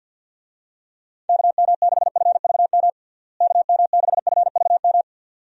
Morse code, heard as a keyed tone, about 49 wpm, 700 Hz. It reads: KM6FVM KM6FVM
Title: USA Call Signs - 6 Characters - Repeated 2x - 50wpm